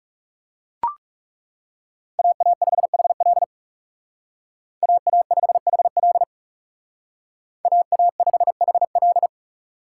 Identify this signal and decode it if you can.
Morse code, heard as a keyed tone, about 36 wpm, 700 Hz. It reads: AA5HL AA5HL AA5HL
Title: Speed Racing - USA 2x2 Call Signs 30wpm